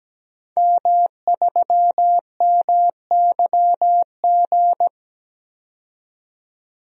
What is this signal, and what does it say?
Morse code, heard as a keyed tone, about 17 wpm, 700 Hz. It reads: M3MYG
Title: DXCC-100 Call Signs - 5 Characters - 17wpm